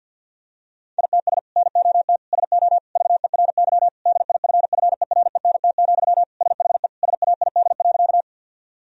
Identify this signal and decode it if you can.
Morse code, heard as a keyed tone, about 50 wpm, 700 Hz. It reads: ITS NOT SO VERY DIFFERENT, SHE SAID.
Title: Sentences from Top 1000 Words - Review Part 3 - 50wpm